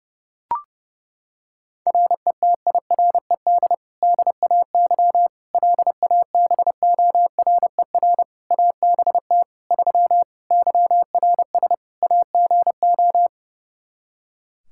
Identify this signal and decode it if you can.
Morse code, heard as a keyed tone, about 30 wpm, 700 Hz. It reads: RETIRED DAY LABORER ABT 3 YRS AGO
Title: Practice QSO - Exchange 3 - More Info - 30wpm